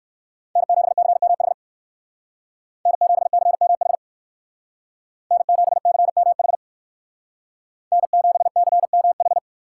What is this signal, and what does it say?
Morse code, heard as a keyed tone, about 59 wpm, 700 Hz. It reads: N7CMH N7CMH N7CMH N7CMH